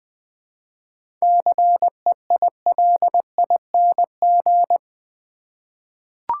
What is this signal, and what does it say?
Morse code, heard as a keyed tone, about 21 wpm, 700 Hz. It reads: CEILING
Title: 7 Letter Words - First 3 Letters Given - 20wpm